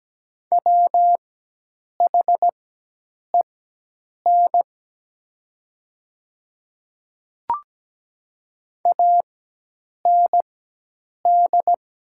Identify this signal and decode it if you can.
Morse code, heard as a keyed tone, about 17 wpm, spaced as for 8 wpm, 700 Hz. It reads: WHEN AND
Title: 4x Character Spacing - Top 100 Words - 17wpm